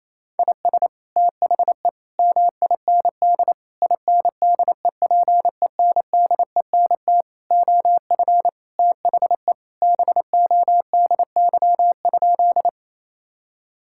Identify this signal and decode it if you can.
Morse code, heard as a keyed tone, about 28 wpm, 700 Hz. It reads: IS THE MIND INDEPENDENT OF THE BODY?
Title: Sentences from Top 1000 Words - Review Part 6 - 28wpm